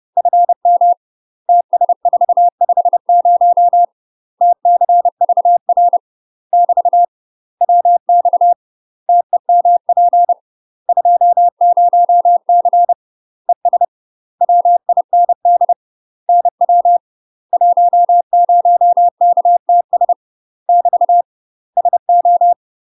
Morse code, heard as a keyed tone, about 30 wpm, 700 Hz. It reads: FM TS450 TCVR <BT> WX TEMP 20C ES WIND NW 10KTS <BT> SO
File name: UK MASTER QSOs TRAINING / 30WPM / UK MASTER QSOs 0018